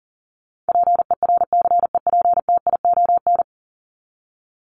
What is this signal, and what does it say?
Morse code, heard as a keyed tone, about 40 wpm, 700 Hz. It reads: PERCEPTION
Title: Top 1751-2000 Words 40wpm